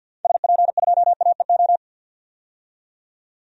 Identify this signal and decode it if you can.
Morse code, heard as a keyed tone, about 50 wpm, 700 Hz. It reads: SP2AEO
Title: DXCC-100 Call Signs - 6 Characters - 50wpm